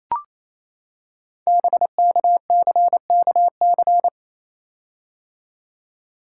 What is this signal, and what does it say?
Morse code, heard as a keyed tone, about 29 wpm, 700 Hz. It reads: BKCKC
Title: Code Groups - 5 Letters - 28wpm